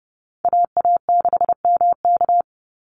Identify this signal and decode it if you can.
Morse code, heard as a keyed tone, about 30 wpm, 700 Hz. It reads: AA6MK
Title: Call Signs - 30wpm